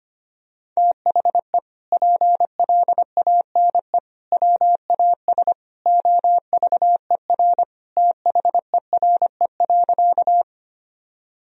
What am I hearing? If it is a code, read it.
Morse code, heard as a keyed tone, about 25 wpm, 700 Hz. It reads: THE PLANE WAS OVER THERE.
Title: Sentences from Top 1000 Words - Review Part 3 - 25wpm